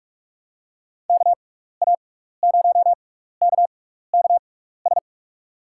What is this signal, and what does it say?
Morse code, heard as a keyed tone, about 47 wpm, spaced as for 16 wpm, 700 Hz. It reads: KA0KKS
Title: USA Call Signs - 6x Character Spacing - 45wpm